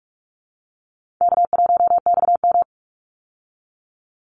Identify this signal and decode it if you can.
Morse code, heard as a keyed tone, about 44 wpm, 700 Hz. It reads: K1XM
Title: Call Signs Repeated 2x - 45wpm